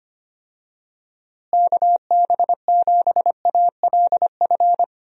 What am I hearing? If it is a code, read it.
Morse code, heard as a keyed tone, about 25 wpm, 700 Hz. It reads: KB7ALF
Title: USA Call Signs - 6 Characters - Repeated 3x - 25wpm